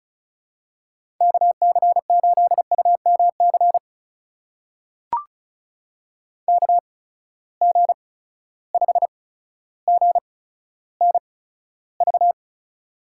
Morse code, heard as a keyed tone, about 35 wpm, spaced as for 15 wpm, 700 Hz. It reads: KC8UMC KG5GNV
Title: USA Call Signs - 6 Characters - 8x Character Spacing - 35wpm